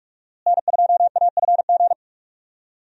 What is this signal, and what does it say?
Morse code, heard as a keyed tone, about 46 wpm, 700 Hz. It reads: N2AFG
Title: USA Call Signs - 5 Characters - Repeated 3x - 45wpm